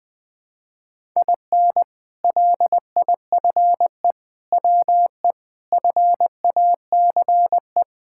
Morse code, heard as a keyed tone, about 20 wpm, 700 Hz. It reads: IN LIFE WE FACE
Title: Sentences with 7-Word Phrases - Spoken Beginning - 20wpm